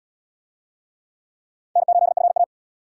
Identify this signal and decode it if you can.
Morse code, heard as a keyed tone, about 67 wpm, 700 Hz. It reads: N8FA